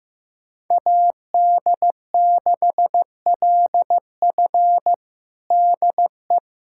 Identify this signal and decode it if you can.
Morse code, heard as a keyed tone, about 15 wpm, 700 Hz. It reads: AD6LF DE
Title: USA Callsign DE Callsign - 5 Characters - 15wpm